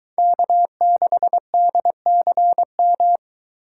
Morse code, heard as a keyed tone, about 22 wpm, 700 Hz. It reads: K6DCM